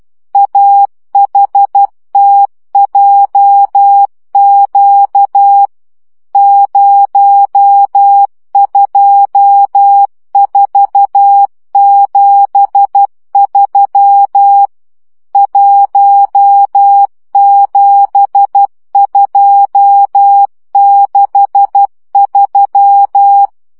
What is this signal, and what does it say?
Morse code, heard as a keyed tone, about 12 wpm, 800 Hz. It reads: AHTJQ 02473 17263
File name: morze vizsga 2011-3 10